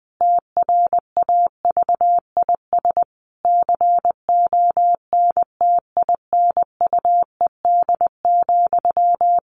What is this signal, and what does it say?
Morse code, heard as a keyed tone, about 20 wpm, 700 Hz. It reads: TRAVIS CONTINUED,
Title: A Sound Of Thunder 20wpm